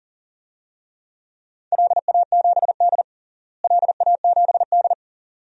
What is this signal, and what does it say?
Morse code, heard as a keyed tone, about 40 wpm, 700 Hz. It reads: LA7D LA7D
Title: DXCC-100 Call Signs - 4 Characters - Repeated 2x - 40wpm